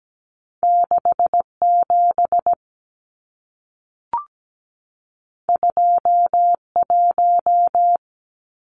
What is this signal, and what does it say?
Morse code, heard as a keyed tone, about 17 wpm, 700 Hz. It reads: 67 21
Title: Numbers - 2-Digit 17wpm